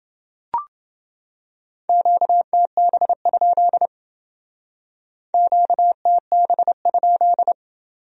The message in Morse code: QTB? QTB?

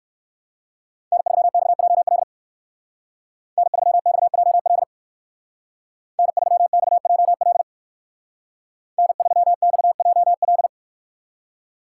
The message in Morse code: N3XJL N3XJL N3XJL N3XJL

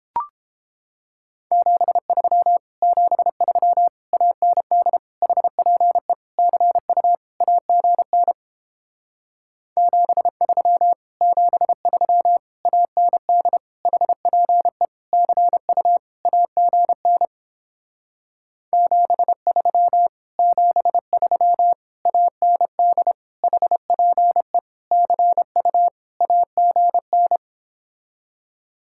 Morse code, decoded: 73 73 AND HPE CU AGN 73 73 AND HPE CU AGN 73 73 AND HPE CU AGN

